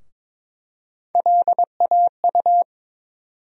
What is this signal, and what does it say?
Morse code, heard as a keyed tone, about 21 wpm, 700 Hz. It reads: LAU
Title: USA Call Sign Postfixes - 3 Letters 22wpm